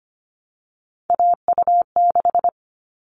Morse code, heard as a keyed tone, about 25 wpm, 700 Hz. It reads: AU6
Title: All Cut Numbers - 3-Digit 25wpm